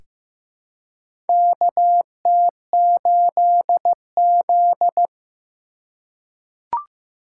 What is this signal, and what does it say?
Morse code, heard as a keyed tone, about 15 wpm, 700 Hz. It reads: KT8Z
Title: USA Call Signs - 4 Characters - 2x Character Spacing - 15wpm